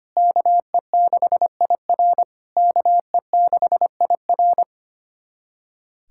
Morse code, heard as a keyed tone, about 25 wpm, 700 Hz. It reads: KE6IR KE6IR
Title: SOTA QSO Practice 25wpm